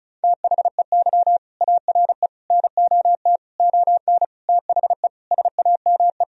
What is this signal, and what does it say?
Morse code, heard as a keyed tone, about 36 wpm, 700 Hz. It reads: THEY ARE NOT ON THE SAME
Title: Sentences from Top 1000 Words - Review Part 4 - 35wpm